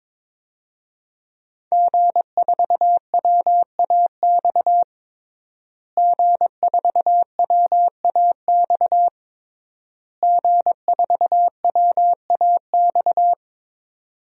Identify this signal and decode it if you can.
Morse code, heard as a keyed tone, about 22 wpm, 700 Hz. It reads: G4WAX G4WAX G4WAX
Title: DXCC-100 Call Signs - 5 Characters - Repeated 3x - 22wpm